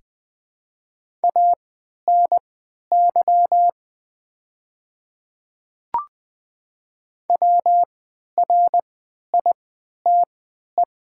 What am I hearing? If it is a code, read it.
Morse code, heard as a keyed tone, about 20 wpm, spaced as for 11 wpm, 700 Hz. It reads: ANY WRITE
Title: 3x Character Spacing - Top 300 Words - 20wpm